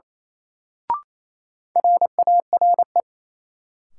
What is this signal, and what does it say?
Morse code, heard as a keyed tone, about 28 wpm, 700 Hz. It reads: RARE
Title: Sentences from Top 1100 Words - New Word Drill - 28wpm